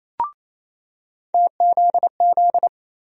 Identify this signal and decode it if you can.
Morse code, heard as a keyed tone, about 28 wpm, 700 Hz. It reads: TZZ
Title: USA Call Sign Postfixes - 3 Letters 28wpm